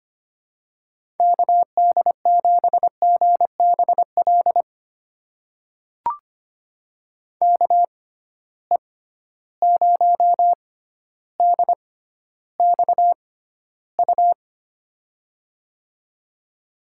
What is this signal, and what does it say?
Morse code, heard as a keyed tone, about 25 wpm, spaced as for 13 wpm, 700 Hz. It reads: KD7GBL KE0DXU
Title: USA Call Signs - 6 Characters - 6x Character Spacing - 25wpm